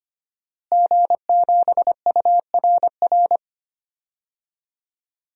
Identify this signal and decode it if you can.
Morse code, heard as a keyed tone, about 25 wpm, 700 Hz. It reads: G7URR
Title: DXCC-100 Call Signs - 5 Characters - Repeated 3x - 25wpm